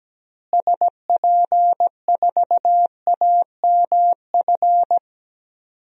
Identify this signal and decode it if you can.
Morse code, heard as a keyed tone, about 17 wpm, 700 Hz. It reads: SP4AMF
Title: DXCC-100 Call Signs - 6 Characters - Repeated 2x - 17wpm